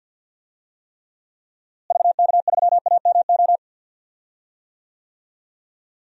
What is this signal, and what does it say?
Morse code, heard as a keyed tone, about 48 wpm, 700 Hz. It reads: VK3AMO
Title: DXCC-100 Call Signs - 6 Characters - Repeated 3x - 50wpm